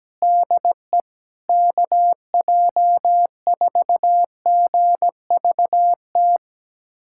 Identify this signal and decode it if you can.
Morse code, heard as a keyed tone, about 17 wpm, 700 Hz. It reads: DE KJ4GVT
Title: USA Callsign DE Callsign - 6 Characters - 17wpm